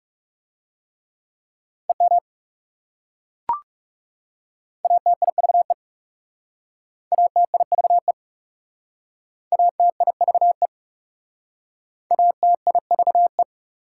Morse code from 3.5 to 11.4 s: ATIVE ATIVE ATIVE

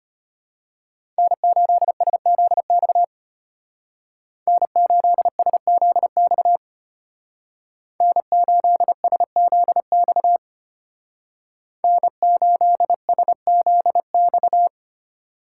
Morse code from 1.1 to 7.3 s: N8SZX N8SZX

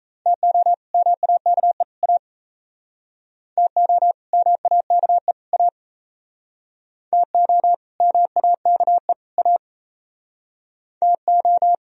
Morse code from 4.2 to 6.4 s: MAKE A

